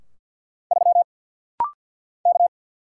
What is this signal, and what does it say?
Morse code, heard as a keyed tone, about 48 wpm, 690 Hz. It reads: 3 K
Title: Single Letter-Number - Rapid-Fire 50wpm